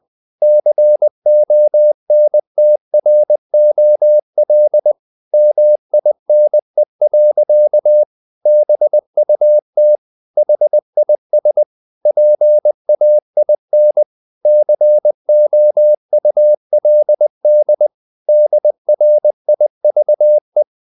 Morse code, heard as a keyed tone, about 20 wpm, 600 Hz. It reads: CONTROL MINE. BUT HIS PAIN COULD DRIVE
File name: the old man and the sea part 2 20wpm